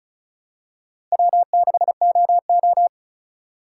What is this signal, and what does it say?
Morse code, heard as a keyed tone, about 36 wpm, 700 Hz. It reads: W6OO
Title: USA Call Signs - 4 Characters - 4x Character Spacing - 35wpm